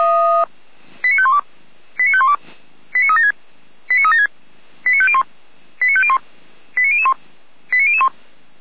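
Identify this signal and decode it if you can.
Five-tone selective call (ZVEI), heard as quick runs of five tones, tones of about 70 ms. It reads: 89512 89512 89376 89376 89501 89501 89001 89001
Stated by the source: ZVEI five-tone selective call received by SDR, hosted by Feuerwehr Orlamünde fire brigade.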